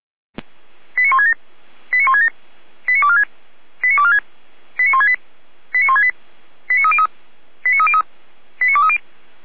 Five-tone selective call (ZVEI), heard as quick runs of five tones, tones of about 70 ms. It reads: 89167 89167 89246 89246 89168 89168 89303 89303 89230